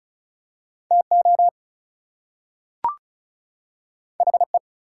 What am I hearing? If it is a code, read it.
Morse code, heard as a keyed tone, about 36 wpm, 700 Hz. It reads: TO HE
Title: Two Letter Words - 35wpm